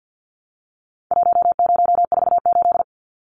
Morse code, heard as a keyed tone, about 50 wpm, 700 Hz. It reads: 1048
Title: Numbers - Contest Style - 4-Digit 50wpm